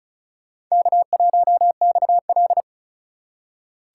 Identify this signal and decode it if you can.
Morse code, heard as a keyed tone, about 34 wpm, 700 Hz. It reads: K1XL
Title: USA Call Signs - 4 Characters - 4x Character Spacing - 35wpm